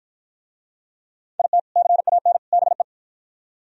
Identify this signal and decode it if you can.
Morse code, heard as a keyed tone, about 50 wpm, 700 Hz. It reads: IT CAN BE